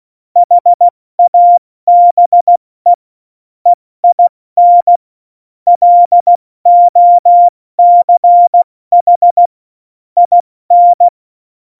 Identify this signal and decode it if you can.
Morse code, heard as a keyed tone, about 16 wpm, 700 Hz. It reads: HABE EIN LOCH IN
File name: Morsekurs / Witze / Witz 16. Oktober 2018, 70 F80